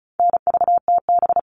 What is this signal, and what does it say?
Morse code, heard as a keyed tone, about 34 wpm, 700 Hz. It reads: NVTB
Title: All Cut Numbers - 4-Digit 35wpm